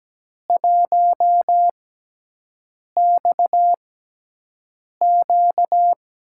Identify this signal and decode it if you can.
Morse code, heard as a keyed tone, about 17 wpm, spaced as for 6 wpm, 700 Hz. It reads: 1XQ
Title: USA Call Signs - 4 Characters - 6x Character Spacing - 17wpm